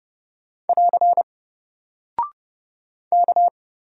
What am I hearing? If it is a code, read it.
Morse code, heard as a keyed tone, about 30 wpm, 700 Hz. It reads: <AR> K
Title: Single Character (Letter:Number:Punctuation:Prosign) 30wpm